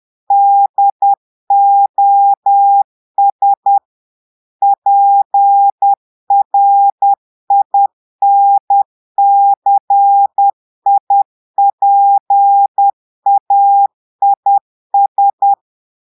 Morse code, heard as a keyed tone, about 10 wpm, 800 Hz. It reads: DOS PRINCIPAIS
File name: Portugais - texte en clair 1 - vitesse - 10 mots minute